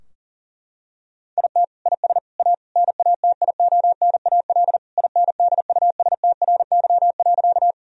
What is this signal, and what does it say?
Morse code, heard as a keyed tone, about 40 wpm, 700 Hz. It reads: IT IS A NATIONAL INDUSTRY.
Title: Sentences from Top 1000 Words - Review Part 4 - 40wpm